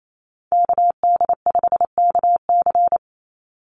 Morse code, heard as a keyed tone, about 28 wpm, 700 Hz. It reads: KD5KC
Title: NA SOTA Chaser Callsigns 28wpm